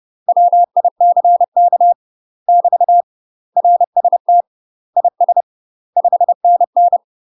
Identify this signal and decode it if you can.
Morse code, heard as a keyed tone, about 30 wpm, 700 Hz. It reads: WICK <BT> RST IS 5NN
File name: UK MASTER QSOs TRAINING / 30WPM / UK MASTER QSOs 0030